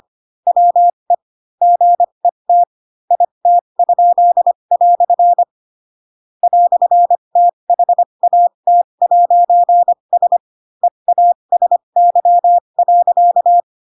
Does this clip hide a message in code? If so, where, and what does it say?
Yes: Morse code from 0.5 to 13.6 s, heard as a keyed tone, about 25 wpm, spaced as for 22 wpm, 700 Hz. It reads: WE GET IT?" "THAT'S EASY.